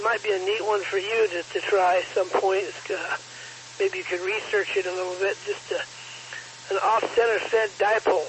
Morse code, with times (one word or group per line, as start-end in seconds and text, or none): none